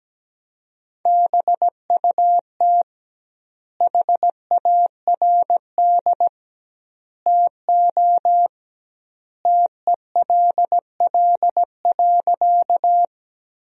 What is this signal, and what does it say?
Morse code, heard as a keyed tone, about 17 wpm, 700 Hz. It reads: BUT HARD TO TELL.
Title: 2x Word-Spacing - Sentences from Top 300 Words - 17wpm